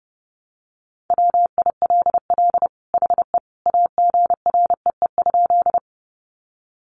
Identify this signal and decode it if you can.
Morse code, heard as a keyed tone, about 30 wpm, 700 Hz. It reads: WILL HE AGREE?
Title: Sentences from Top 1100 Words 30wpm